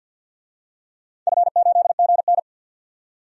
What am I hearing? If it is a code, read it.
Morse code, heard as a keyed tone, about 46 wpm, 700 Hz. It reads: F8GN
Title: DXCC-100 Call Signs - 4 Characters - 50wpm